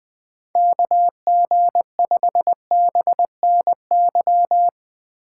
Morse code, heard as a keyed tone, about 20 wpm, 700 Hz. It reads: KG5BNY